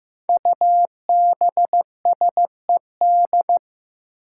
Morse code, heard as a keyed tone, about 15 wpm, 700 Hz. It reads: UBSED
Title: Five Challenging Letter-Number Combinations 15wpm